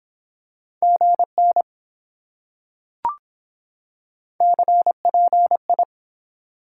Morse code, heard as a keyed tone, about 23 wpm, 700 Hz. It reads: GN CPI